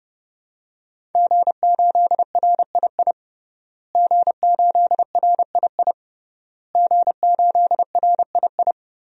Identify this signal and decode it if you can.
Morse code, heard as a keyed tone, about 30 wpm, 700 Hz. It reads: G8RII G8RII G8RII
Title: DXCC-100 Call Signs - 5 Characters - Repeated 3x - 30wpm